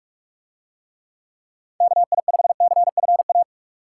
Morse code, heard as a keyed tone, about 49 wpm, 700 Hz. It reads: KI5CFA